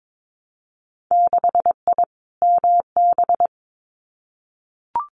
Morse code, heard as a keyed tone, about 22 wpm, 700 Hz. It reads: 6I MB
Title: Winter Field Day Exchanges Encore 22wpm